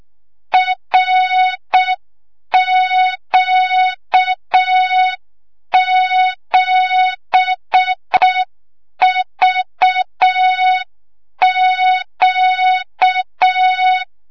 Morse code, heard as a keyed tone, about 6 wpm, 3700 Hz, with noise. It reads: RQ7VQ
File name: Morze-HA2MN / hang / 25 GYAKORLAT